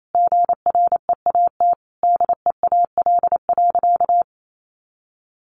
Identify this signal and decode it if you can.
Morse code, heard as a keyed tone, about 28 wpm, 700 Hz. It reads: GREAT DEAL.
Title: Sentences from Top 1300 Words 28wpm